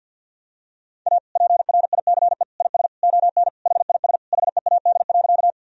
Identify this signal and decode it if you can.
Morse code, heard as a keyed tone, about 50 wpm, 700 Hz. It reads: A PRICE IS ON HIS HEAD.
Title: Sentences from Top 1000 Words - Review Part 4 - 50wpm